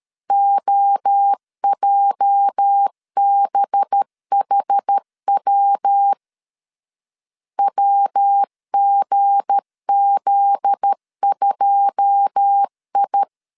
Morse code, heard as a keyed tone, about 13 wpm, 800 Hz. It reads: OJBHW WGZ2I
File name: CW / DET / 1 9 1500C S C 12wpm 3811